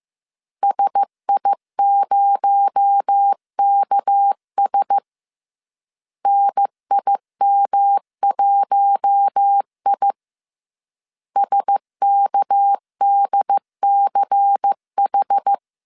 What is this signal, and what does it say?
Morse code, heard as a keyed tone, about 15 wpm, 800 Hz. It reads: SI0KS NIM1I SKDCH